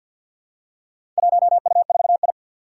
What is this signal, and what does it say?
Morse code, heard as a keyed tone, about 52 wpm, 700 Hz. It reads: 1U4I
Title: Four Challenging Letter-Number Combinations 50wpm